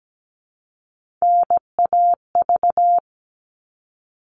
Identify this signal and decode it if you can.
Morse code, heard as a keyed tone, about 17 wpm, 700 Hz. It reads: NAV
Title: All Cut Numbers - 3-Digit 17wpm